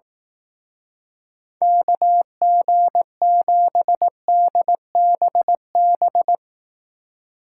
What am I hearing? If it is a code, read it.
Morse code, heard as a keyed tone, about 18 wpm, 700 Hz. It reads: KG7DBB
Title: Speed Racing - USA 2x3 Call Signs 15wpm